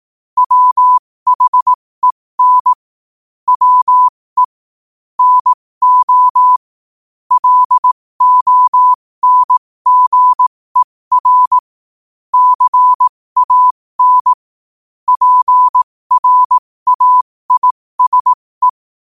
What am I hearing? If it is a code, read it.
Morse code, heard as a keyed tone, about 18 wpm, spaced as for 15 wpm, 1000 Hz. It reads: WHEN WE NO LONGER CAN PRAISE
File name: qotdfiles / 15wpm